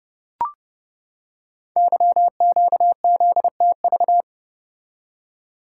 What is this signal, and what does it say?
Morse code, heard as a keyed tone, about 30 wpm, 700 Hz. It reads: YQZTV
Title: Code Groups - 5 Letter:Numbers - 30wpm